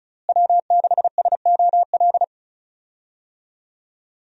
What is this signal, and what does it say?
Morse code, heard as a keyed tone, about 36 wpm, 700 Hz. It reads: W6SOL
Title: USA Call Signs - 5 Characters - Repeated 3x - 35wpm